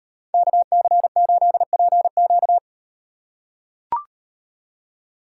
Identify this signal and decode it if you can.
Morse code, heard as a keyed tone, about 37 wpm, 700 Hz. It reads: KC8PQ